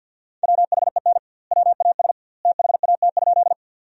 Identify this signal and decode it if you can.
Morse code, heard as a keyed tone, about 50 wpm, 700 Hz. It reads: WHEN WAS THAT?
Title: 2x Word-Spacing - Sentences from Top 100 Words - 50wpm